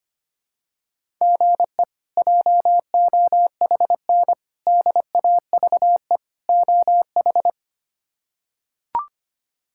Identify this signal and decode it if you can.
Morse code, heard as a keyed tone, about 25 wpm, 700 Hz. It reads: GE JOHN DAVE OH
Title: K1USN SST QSO Practice - 25wpm